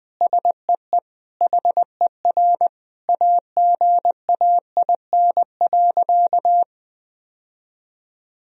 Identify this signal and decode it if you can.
Morse code, heard as a keyed tone, about 20 wpm, 700 Hz. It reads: SEE HER AGAIN.
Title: Sentences from Top 1000 Words - Review Part 3 - 20wpm